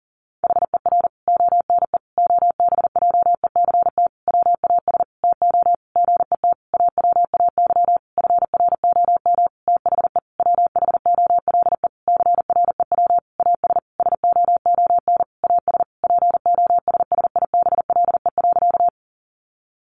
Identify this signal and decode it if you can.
Morse code, heard as a keyed tone, about 40 wpm, 700 Hz. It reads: HER ONE OBJECT WAS TO GET AWAY FROM THE WHOLE CREW AS SOON AS POSSIBLE.
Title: Sentences from Top 1500 Words 40wpm